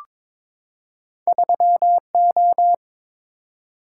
Morse code, heard as a keyed tone, about 22 wpm, 690 Hz. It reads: 3O
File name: practice / 22wpm / Two Letter-Number Combinations - 22wpm